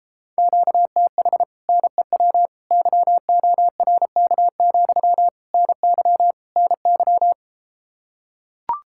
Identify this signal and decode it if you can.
Morse code, heard as a keyed tone, about 34 wpm, 700 Hz. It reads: QTH NEW YORK, NY NY